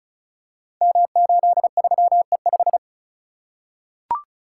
Morse code, heard as a keyed tone, about 36 wpm, 700 Hz. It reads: M83E5
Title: Code Groups - 5 Letter:Numbers - 35wpm